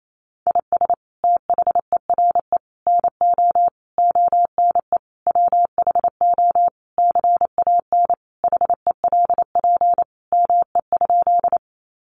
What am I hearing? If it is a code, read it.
Morse code, heard as a keyed tone, about 28 wpm, 700 Hz. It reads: IS THERE NO ONE WHO CAN HELP ME?
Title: Sentences from Top 200 Words 28wpm